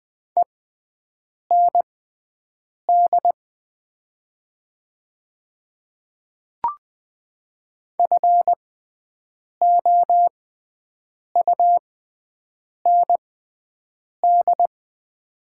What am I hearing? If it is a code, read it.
Morse code, heard as a keyed tone, about 20 wpm, spaced as for 7 wpm, 700 Hz. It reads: END FOUND
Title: 6x Character Spacing - Top 300 Words - 20wpm